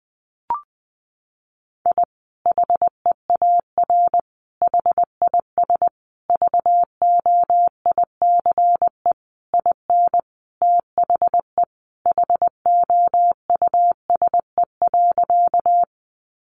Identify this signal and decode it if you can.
Morse code, heard as a keyed tone, about 20 wpm, 700 Hz. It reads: I HEAR HIS VOICE IN THE HOUSE.
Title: Sentences from Top 500 Words 20wpm